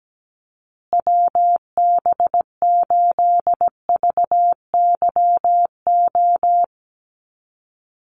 Morse code, heard as a keyed tone, about 17 wpm, 700 Hz. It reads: WB8VYO
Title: USA Call Signs Repeated 3x 17wpm